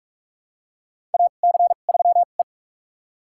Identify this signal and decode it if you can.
Morse code, heard as a keyed tone, about 41 wpm, spaced as for 31 wpm, 690 Hz. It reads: AC3E
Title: USA Call Signs - 4 Characters - 2x Character Spacing - 45wpm